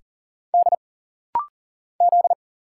Morse code, heard as a keyed tone, about 40 wpm, 700 Hz. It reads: D Z
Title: Single Letter-Number Alt - Rapid-Fire 40wpm